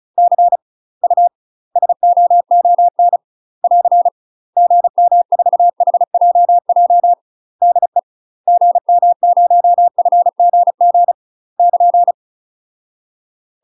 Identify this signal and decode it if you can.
Morse code, heard as a keyed tone, about 36 wpm, 700 Hz. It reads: C U SOON <AR> GM4HJJ DE GM0FGG <KN>
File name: UK MASTER QSOs TRAINING / 35WPM / UK MASTER QSOs 0029